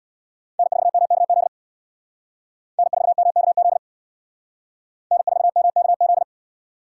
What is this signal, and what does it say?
Morse code, heard as a keyed tone, about 64 wpm, 700 Hz. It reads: N4MKZ N4MKZ N4MKZ